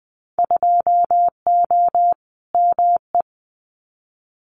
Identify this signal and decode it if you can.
Morse code, heard as a keyed tone, about 20 wpm, 700 Hz. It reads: 2O ME
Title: Winter Field Day Exchanges 20wpm